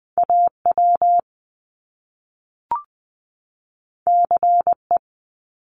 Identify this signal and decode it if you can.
Morse code, heard as a keyed tone, about 20 wpm, 700 Hz. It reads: AW CE
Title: Top Two Letter Combinations in Words 20wpm